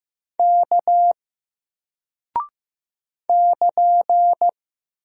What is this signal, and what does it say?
Morse code, heard as a keyed tone, about 15 wpm, 700 Hz. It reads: K <KN>
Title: Single Character (Letter:Number:Punctuation:Prosign) Alt 15wpm